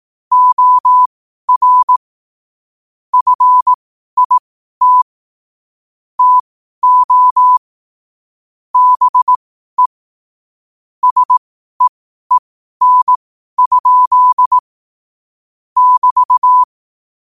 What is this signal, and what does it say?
Morse code, heard as a keyed tone, about 18 wpm, spaced as for 12 wpm, 1000 Hz. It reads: OR FIT TO BE SEEN? <BT>
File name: qotdfiles / 12wpm